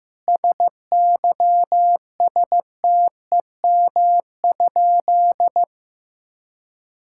Morse code, heard as a keyed tone, about 15 wpm, 700 Hz. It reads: SYSTEM?
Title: Sentences from Top 1000 Words - Review Part 5 - 15wpm